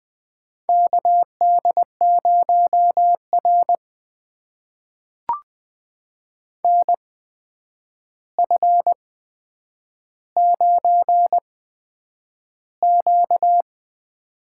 Morse code, heard as a keyed tone, about 20 wpm, spaced as for 9 wpm, 700 Hz. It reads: KD0R NF9Q
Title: USA Call Signs - 4 Characters - 8x Character Spacing - 20wpm